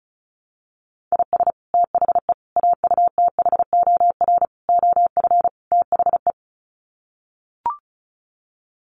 Morse code, heard as a keyed tone, about 34 wpm, 700 Hz. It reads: IS THE AUTHOR OF THE
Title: Sets of 5 Words Encore 35wpm